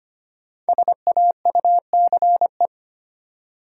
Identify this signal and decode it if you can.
Morse code, heard as a keyed tone, about 25 wpm, 700 Hz. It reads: SAUCE
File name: 5 Letter Words - First 3 Letters Given - 25wpm